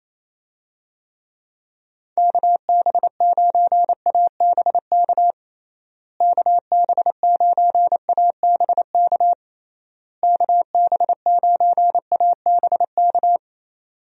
Morse code, heard as a keyed tone, about 27 wpm, 700 Hz. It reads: KB9ABK KB9ABK KB9ABK
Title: USA Call Signs - 6 Characters - Repeated 3x - 28wpm